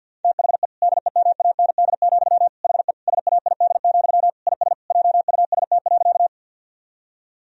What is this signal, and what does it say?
Morse code, heard as a keyed tone, about 50 wpm, 700 Hz. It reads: THE DEMAND, HE SAID, IS JUST.
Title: Sentences from Top 1000 Words - Review Part 4 - 50wpm